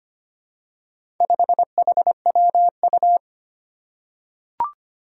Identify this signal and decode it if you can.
Morse code, heard as a keyed tone, about 25 wpm, 700 Hz. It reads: 5HWU
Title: Four Challenging Letter-Number Combinations 25wpm